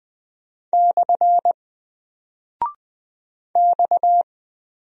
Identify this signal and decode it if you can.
Morse code, heard as a keyed tone, about 20 wpm, 700 Hz. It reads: / X
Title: Single Character (Letter:Number:Punctuation:Prosign) 20wpm